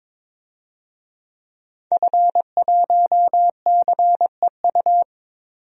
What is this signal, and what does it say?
Morse code, heard as a keyed tone, about 22 wpm, 700 Hz. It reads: F1CEU
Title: DXCC-100 Call Signs - 5 Characters - 22wpm